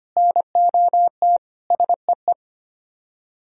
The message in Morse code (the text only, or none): NOT SEE